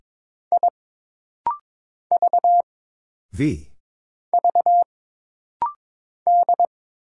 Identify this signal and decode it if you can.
Morse code, heard as a keyed tone, about 22 wpm, 700 Hz. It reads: I V V D
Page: This rapid-fire timing is geared to push your ICR (Instant Character Recognition) skills.